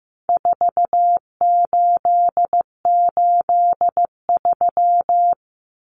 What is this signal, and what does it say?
Morse code, heard as a keyed tone, about 15 wpm, 700 Hz. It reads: 4883
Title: Numbers - 4-Digit 15wpm